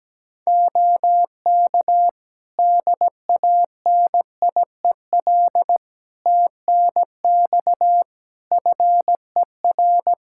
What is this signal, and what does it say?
Morse code, heard as a keyed tone, about 17 wpm, 700 Hz. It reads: OK DANIEL TNX FER
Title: K1USN SST QSO Practice 17wpm